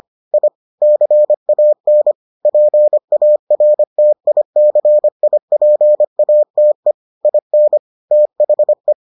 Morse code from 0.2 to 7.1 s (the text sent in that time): I CAN PARTICIPATE